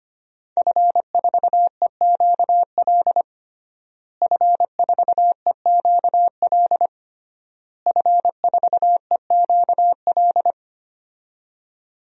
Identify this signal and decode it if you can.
Morse code, heard as a keyed tone, about 25 wpm, 700 Hz. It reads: F4EQL F4EQL F4EQL
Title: DXCC-100 Call Signs - 5 Characters - Repeated 3x - 25wpm